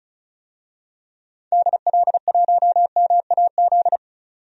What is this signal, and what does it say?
Morse code, heard as a keyed tone, about 34 wpm, 700 Hz. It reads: DL1MAZ
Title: DXCC-100 Call Signs - 6 Characters - 35wpm